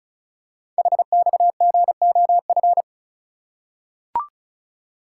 Morse code, heard as a keyed tone, about 36 wpm, 690 Hz. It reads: HXGOF
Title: Code Groups - 5 Letters - 35wpm